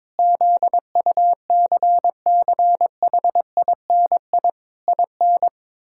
Morse code, heard as a keyed tone, about 21 wpm, 700 Hz. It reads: ZUCCHINI IN
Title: Pangram Challenge v2 - 22wpm